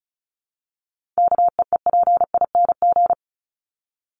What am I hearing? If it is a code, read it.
Morse code, heard as a keyed tone, about 34 wpm, 700 Hz. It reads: KEEPING
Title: ING Suffix - Top 100 Words with Base Words 35wpm